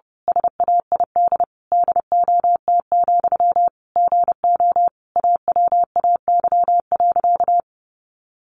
Morse code, heard as a keyed tone, about 30 wpm, 700 Hz. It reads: SAID DOT, GO AWAY.
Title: Sentences from Top 1500 Words 30wpm